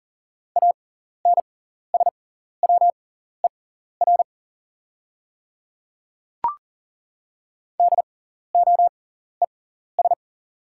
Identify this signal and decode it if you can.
Morse code, heard as a keyed tone, about 40 wpm, spaced as for 14 wpm, 700 Hz. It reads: ANSWER DOES
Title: 6x Character Spacing - Top 300 Words - 40wpm